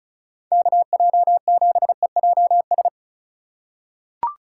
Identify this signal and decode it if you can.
Morse code, heard as a keyed tone, about 36 wpm, 700 Hz. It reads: KJ7EJS
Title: USA Call Signs - 6 Characters - 3x Character Spacing - 35wpm